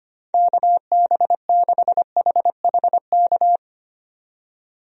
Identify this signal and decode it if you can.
Morse code, heard as a keyed tone, about 25 wpm, 700 Hz. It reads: KB6HHK
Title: USA Call Signs - 6 Characters - 25wpm